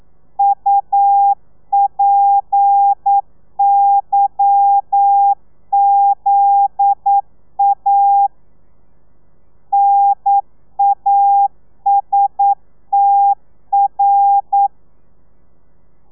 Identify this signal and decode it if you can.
Morse code, heard as a keyed tone, about 9 wpm, 800 Hz. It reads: UPYZA NASTR
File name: telegrafia / i2tfj / Lez11 40